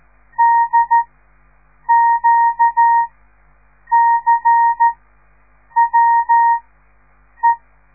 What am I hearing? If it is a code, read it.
Morse code, heard as a keyed tone, about 13 wpm, spaced as for 7 wpm, 930 Hz. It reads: DQCWE
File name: radiomorningstar / morse / les 6